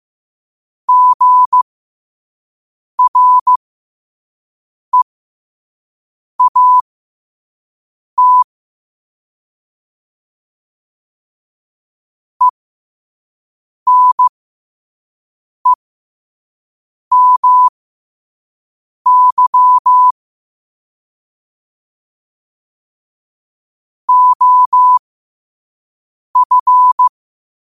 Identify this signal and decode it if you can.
Morse code, heard as a keyed tone, about 15 wpm, spaced as for 5 wpm, 1000 Hz. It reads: GREAT ENEMY OF
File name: qotdfiles / 5wpm